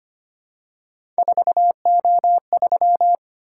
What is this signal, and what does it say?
Morse code, heard as a keyed tone, about 25 wpm, 700 Hz. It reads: 4O3
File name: Three Letter-Number Combinations - 25wpm